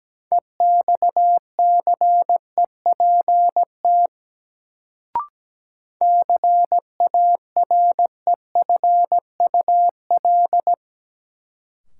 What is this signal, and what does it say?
Morse code, heard as a keyed tone, about 17 wpm, 700 Hz. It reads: EXCEPT CAREFUL
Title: Sentences from Top 1600 Words - New Word Drill Alt - 17wpm